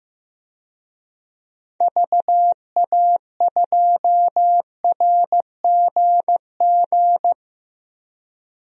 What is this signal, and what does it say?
Morse code, heard as a keyed tone, about 15 wpm, 700 Hz. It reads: VA2RGG
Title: DXCC-100 Call Signs - 6 Characters - Repeated 2x - 15wpm